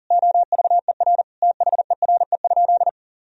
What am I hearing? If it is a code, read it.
Morse code, heard as a keyed tone, about 40 wpm, 700 Hz. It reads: OVER THERE?
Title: Sentences from Top 1000 Words - Review Part 2 - 40wpm